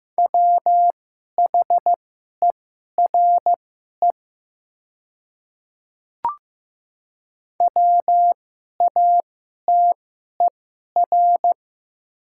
Morse code, heard as a keyed tone, about 15 wpm, spaced as for 11 wpm, 700 Hz. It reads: WHERE WATER
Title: 2x Character Spacing - Top 200 Words - 15wpm